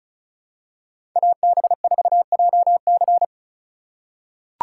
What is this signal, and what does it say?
Morse code, heard as a keyed tone, about 36 wpm, 700 Hz. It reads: AB4JC
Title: USA Call Signs - 2x Character Spacing - 35wpm